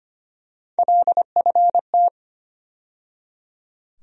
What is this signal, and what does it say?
Morse code, heard as a keyed tone, about 25 wpm, 700 Hz. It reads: LFT
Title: USA Call Sign Postfixes - 3 Letters 25wpm